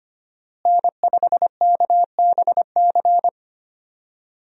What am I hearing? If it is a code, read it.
Morse code, heard as a keyed tone, about 25 wpm, 700 Hz. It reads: N5KBC
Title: USA Call Signs - 5 Characters - Repeated 3x - 25wpm